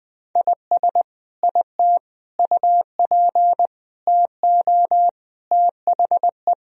Morse code, heard as a keyed tone, about 21 wpm, 700 Hz. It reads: IS IT UP TO THE
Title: Sentences from Top 1000 Words - Review Part 4 - 20wpm